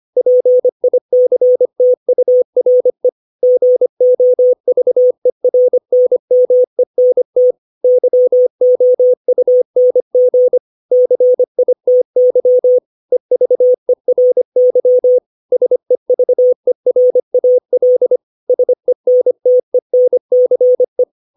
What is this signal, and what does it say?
Morse code, heard as a keyed tone, about 25 wpm, 500 Hz. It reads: PICTURE GOVERNMENT YOUNG CITY EVERY SEVERAL SENTENCE
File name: 632 words / 632 Words 25WPM